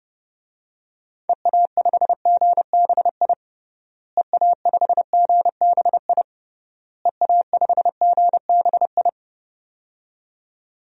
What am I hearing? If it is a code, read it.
Morse code, heard as a keyed tone, about 32 wpm, 700 Hz. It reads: EA5GBI EA5GBI EA5GBI
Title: DXCC-100 Call Signs - 6 Characters - Repeated 3x - 30wpm